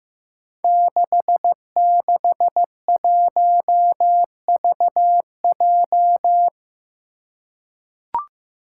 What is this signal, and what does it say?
Morse code, heard as a keyed tone, about 15 wpm, 700 Hz. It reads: 661VJ
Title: Five Challenging Letter-Number Combinations 15wpm